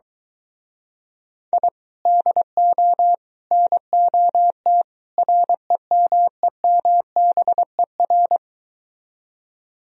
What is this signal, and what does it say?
Morse code, heard as a keyed tone, about 23 wpm, 700 Hz. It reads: I DO NOT REMEMBER